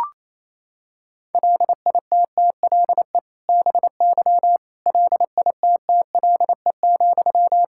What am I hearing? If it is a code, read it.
Morse code, heard as a keyed tone, about 28 wpm, 700 Hz. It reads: LITTLE BY LITTLE,
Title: Sentences from Top 1600 Words 28wpm